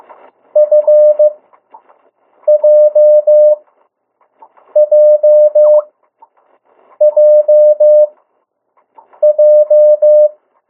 Morse code, heard as a keyed tone, about 15 wpm, spaced as for 6 wpm, 600 Hz. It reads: FJJJJ
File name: morse mp3 / 02 urok 1